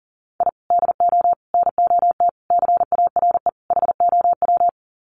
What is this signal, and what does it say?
Morse code, heard as a keyed tone, about 40 wpm, 700 Hz. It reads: I DO NOT CARE HOW
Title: Sets of 5 Words Encore2 40wpm